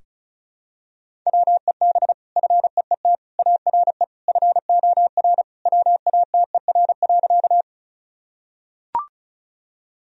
Morse code, heard as a keyed tone, about 36 wpm, 700 Hz. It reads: WEB FEET ARE FOR WATER.
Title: Sentences from Top 1000 Words - Review Part 4 - 35wpm